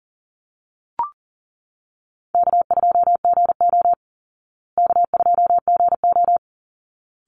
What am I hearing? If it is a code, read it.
Morse code, heard as a keyed tone, about 40 wpm, 700 Hz. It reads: K2GO K2GO
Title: Call Signs Repeated 3x - 40wpm